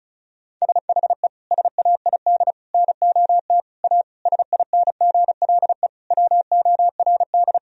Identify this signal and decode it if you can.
Morse code, heard as a keyed tone, about 36 wpm, 700 Hz. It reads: SHE SAID NOT A SINGLE WORD
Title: Sentences from Top 1000 Words - Review Part 4 - 35wpm